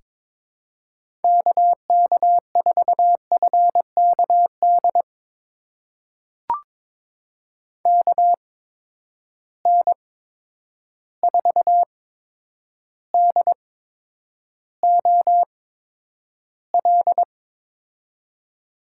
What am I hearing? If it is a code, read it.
Morse code, heard as a keyed tone, about 22 wpm, spaced as for 9 wpm, 700 Hz. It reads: KK4FKD KN4DOL
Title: USA Call Signs - 6 Characters - 8x Character Spacing - 22wpm